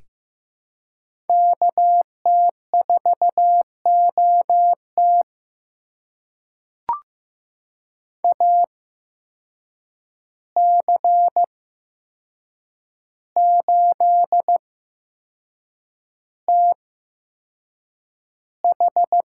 Morse code, heard as a keyed tone, about 15 wpm, spaced as for 6 wpm, 700 Hz. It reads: KT4OT AC8TH
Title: USA Call Signs - 5 Characters - 8x Character Spacing - 15wpm